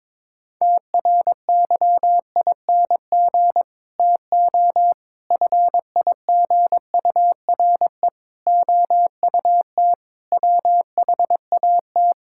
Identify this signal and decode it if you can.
Morse code, heard as a keyed tone, about 22 wpm, 700 Hz. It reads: TRYING TO FIGURE OUT WHAT
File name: practice / 22wpm / Build A Phrase - 5-Word Phrases - 22wpm